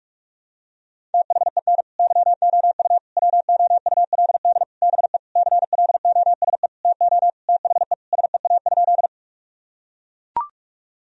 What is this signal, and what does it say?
Morse code, heard as a keyed tone, about 41 wpm, 700 Hz. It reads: THEN YOU WOULD BE CLOSE TO THE SEA?
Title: Sentences from Top 1000 Words - Review Part 1 - 45wpm